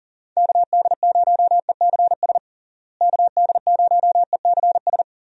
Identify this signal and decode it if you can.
Morse code, heard as a keyed tone, about 40 wpm, 700 Hz. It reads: KD0ECS KD0ECS
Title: USA Call Signs - 6 Characters - Repeated 3x - 40wpm